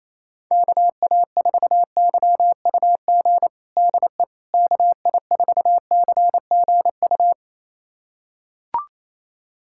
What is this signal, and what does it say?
Morse code, heard as a keyed tone, about 28 wpm, 700 Hz. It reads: KA4YUG DE KI4CGU
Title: USA Callsign DE Callsign - 6 Characters - 28wpm